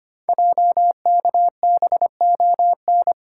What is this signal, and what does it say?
Morse code, heard as a keyed tone, about 25 wpm, 690 Hz. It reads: JKBON
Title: Code Groups - 5 Letter:Numbers - 25wpm